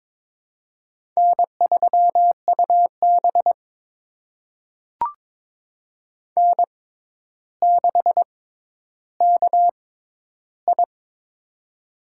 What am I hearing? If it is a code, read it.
Morse code, heard as a keyed tone, about 22 wpm, spaced as for 11 wpm, 700 Hz. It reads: N3UB N6KI
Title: USA Call Signs - 4 Characters - 6x Character Spacing - 22wpm